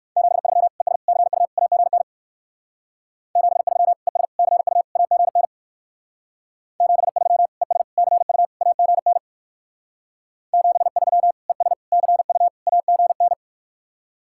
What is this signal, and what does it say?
Morse code, heard as a keyed tone, about 54 wpm, 700 Hz. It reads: 73 ES CU AGN 73 ES CU AGN 73 ES CU AGN 73 ES CU AGN
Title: Speed-Racing Common QSO Phrases - 45wpm